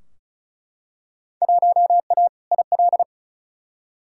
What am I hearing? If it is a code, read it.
Morse code, heard as a keyed tone, about 34 wpm, 700 Hz. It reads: 1A IL
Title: Field Day Exchanges - 35wpm